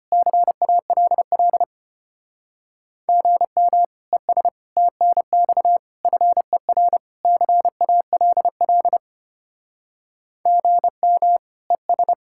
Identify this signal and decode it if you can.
Morse code, heard as a keyed tone, about 31 wpm, 700 Hz. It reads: CALL GM ES TNX FER CALL GM ES
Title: Speed-Racing Common QSO Phrases - 25wpm